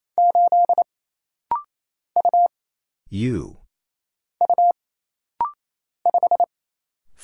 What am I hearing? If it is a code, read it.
Morse code, heard as a keyed tone, about 27 wpm, 700 Hz. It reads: 8 U U 5
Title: Single Letter-Number - Rapid-Fire 28wpm